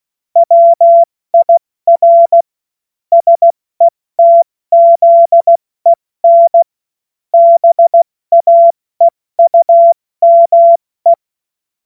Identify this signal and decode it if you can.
Morse code, heard as a keyed tone, about 16 wpm, 680 Hz. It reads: WIR SETZEN BAEUME
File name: Morsekurs / Witze / Witz 22. August 2023 70 F80